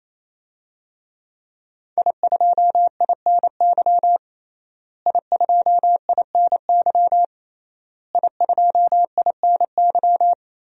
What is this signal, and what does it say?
Morse code, heard as a keyed tone, about 28 wpm, 700 Hz. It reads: I2INY I2INY I2INY
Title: DXCC-100 Call Signs - 5 Characters - Repeated 3x - 28wpm